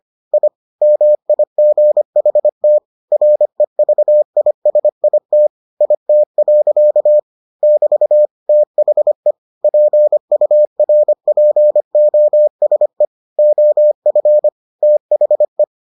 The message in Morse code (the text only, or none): I MIGHT REVISIT IT. <BT> THE PURPOSE OF THE